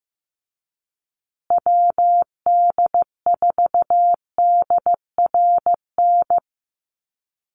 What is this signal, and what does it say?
Morse code, heard as a keyed tone, about 15 wpm, 700 Hz. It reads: WD4DRN
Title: USA Call Signs Repeated 2x 15wpm